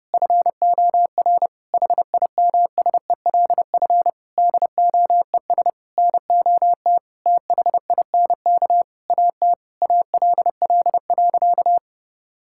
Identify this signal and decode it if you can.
Morse code, heard as a keyed tone, about 30 wpm, 700 Hz. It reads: FOR HIMSELF DOES NOT THINK AT ALL.
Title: Sentences from Top 1600 Words 30wpm